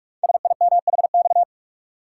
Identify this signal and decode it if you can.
Morse code, heard as a keyed tone, about 44 wpm, 690 Hz. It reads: SIMHX
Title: Code Groups - 5 Letters - 45wpm